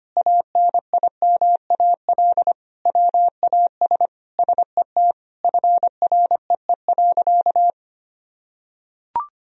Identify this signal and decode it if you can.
Morse code, heard as a keyed tone, about 25 wpm, 700 Hz. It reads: ANIMAL WAS SET FREE.
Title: Sentences from Top 1000 Words - Review Part 1 - 25wpm